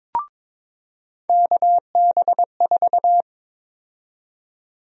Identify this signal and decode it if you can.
Morse code, heard as a keyed tone, about 22 wpm, 700 Hz. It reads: KB4
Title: USA Call Sign Prefixes - 22wpm